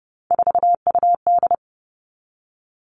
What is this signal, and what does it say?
Morse code, heard as a keyed tone, about 30 wpm, 700 Hz. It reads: 4UD
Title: All Cut Numbers - 3-Digit 30wpm